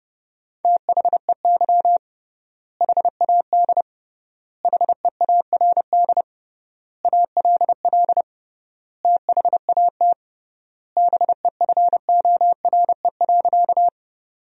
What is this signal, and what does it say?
Morse code, heard as a keyed tone, about 32 wpm, 700 Hz. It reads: THEY HAD HEARD ALL THAT BEFORE.
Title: 3x Word-Spacing - Sentences from Top 400 Words - 30wpm